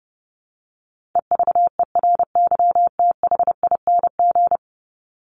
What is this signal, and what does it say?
Morse code, heard as a keyed tone, about 30 wpm, 700 Hz. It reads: EVERYTHING
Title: Top 501-750 Words 30wpm